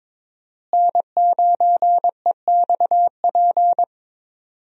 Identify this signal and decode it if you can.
Morse code, heard as a keyed tone, about 22 wpm, 700 Hz. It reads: N9EXP
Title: USA Call Signs - 5 Characters - Repeated 3x - 22wpm